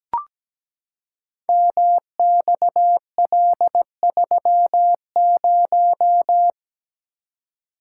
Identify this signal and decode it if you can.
Morse code, heard as a keyed tone, about 17 wpm, 700 Hz. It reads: MXL30
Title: Code Groups - 5 Letter:Numbers - 17wpm